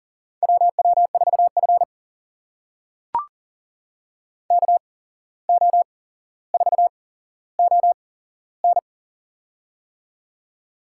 Morse code, heard as a keyed tone, about 38 wpm, spaced as for 16 wpm, 700 Hz. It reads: WW4F KO4ON
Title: USA Call Signs - 8x Character Spacing - 40wpm